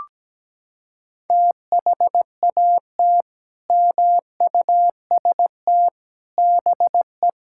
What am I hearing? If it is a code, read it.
Morse code, heard as a keyed tone, about 17 wpm, 700 Hz. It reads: THAT MUST BE
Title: Sentences from Top 1000 Words - Review Part 5 - 17wpm